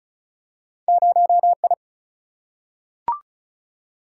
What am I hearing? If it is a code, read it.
Morse code, heard as a keyed tone, about 36 wpm, 700 Hz. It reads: 0I
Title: Single-Letter Single-Number 35wpm